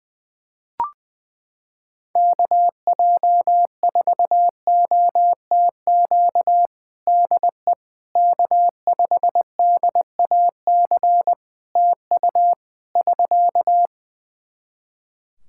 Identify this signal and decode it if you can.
Morse code, heard as a keyed tone, about 21 wpm, 700 Hz. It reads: KJ4OTQ DE K5DAC TU <SK>
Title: Practice QSO - Exchange 4 - Ending - 20wpm